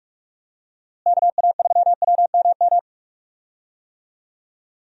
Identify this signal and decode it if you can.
Morse code, heard as a keyed tone, about 46 wpm, 700 Hz. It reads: KA3WMM
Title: USA Call Signs - 6 Characters - Repeated 2x - 45wpm